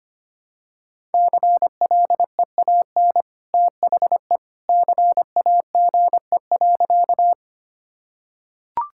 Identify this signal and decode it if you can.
Morse code, heard as a keyed tone, about 25 wpm, 700 Hz. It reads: CLEAN THE CAGE.
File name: Easy Sentences with Hints - 25wpm